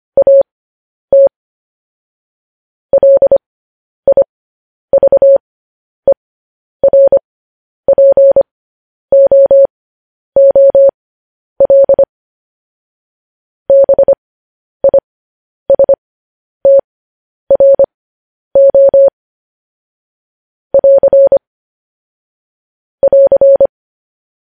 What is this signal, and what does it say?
Morse code, heard as a keyed tone, about 25 wpm, spaced as for 10 wpm, 550 Hz. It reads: AT LIVERPOOL BISTRO <AR> <AR>